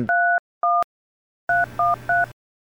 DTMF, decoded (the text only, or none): 31 313